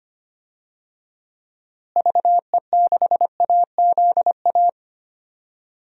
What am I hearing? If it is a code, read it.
Morse code, heard as a keyed tone, about 25 wpm, 700 Hz. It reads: VE6AZA
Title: DXCC-100 Call Signs - 6 Characters - Repeated 3x - 25wpm